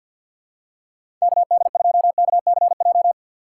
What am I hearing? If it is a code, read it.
Morse code, heard as a keyed tone, about 48 wpm, 700 Hz. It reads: KD2KCJ
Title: USA Call Signs - 4x Character Spacing - 50wpm